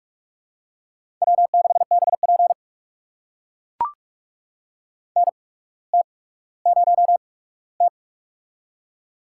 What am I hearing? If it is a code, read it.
Morse code, heard as a keyed tone, about 44 wpm, spaced as for 19 wpm, 700 Hz. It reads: W6BP NT0T
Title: USA Call Signs - 4 Characters - 8x Character Spacing - 45wpm